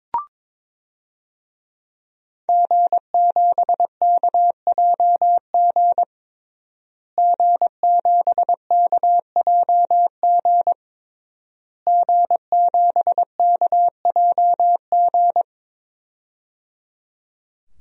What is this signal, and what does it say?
Morse code, heard as a keyed tone, about 22 wpm, 700 Hz. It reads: G7KJG G7KJG G7KJG
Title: DXCC-100 Call Signs - 5 Characters - Repeated 3x - 22wpm